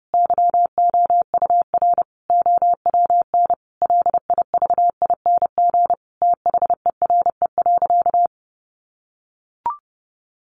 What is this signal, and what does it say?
Morse code, heard as a keyed tone, about 30 wpm, 700 Hz. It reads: YOUR OWN LIVING THERE.
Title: Sentences from Top 1500 Words 30wpm